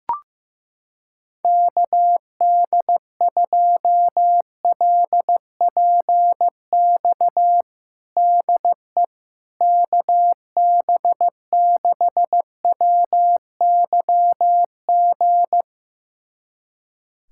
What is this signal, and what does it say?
Morse code, heard as a keyed tone, about 16 wpm, 700 Hz. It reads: KD2LPX DE KB6WYG
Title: USA Callsign DE Callsign - 6 Characters - 15wpm